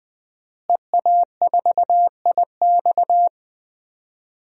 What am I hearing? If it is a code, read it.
Morse code, heard as a keyed tone, about 19 wpm, 700 Hz. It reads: EA4IX
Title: DXCC-100 Call Signs - 5 Characters - Repeated 3x - 20wpm